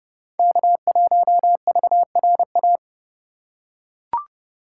Morse code, heard as a keyed tone, about 30 wpm, 700 Hz. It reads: K1VRA